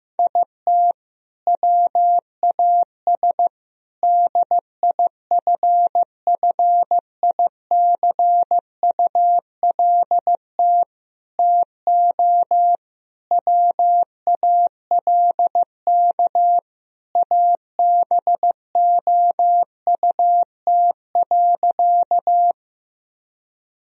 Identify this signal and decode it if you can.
Morse code, heard as a keyed tone, about 15 wpm, 700 Hz. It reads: IT WAS DIFFICULT TO WALK ABOUT.
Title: Sentences from Top 1000 Words - Review Part 1 - 15wpm